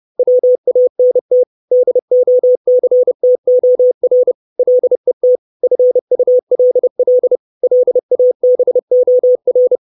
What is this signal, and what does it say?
Morse code, heard as a keyed tone, about 30 wpm, 500 Hz. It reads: WANT DOCTOR LET FULL LABOR
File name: 632 Words 30WPM